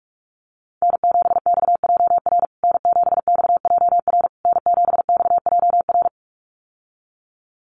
Morse code, heard as a keyed tone, about 44 wpm, 700 Hz. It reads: N7XJR N7XJR N7XJR
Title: USA Call Signs Repeated 3x 45wpm